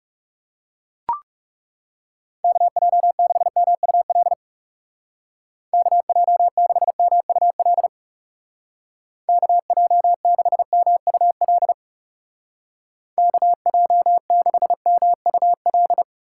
Morse code, heard as a keyed tone, about 39 wpm, 700 Hz. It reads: KJ6MUL KJ6MUL KJ6MUL KJ6MUL